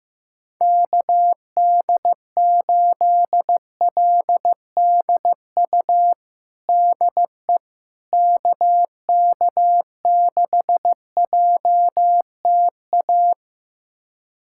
Morse code, heard as a keyed tone, about 16 wpm, 700 Hz. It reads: KD8LDU DE KK6JTA
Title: USA Callsign DE Callsign - 6 Characters - 15wpm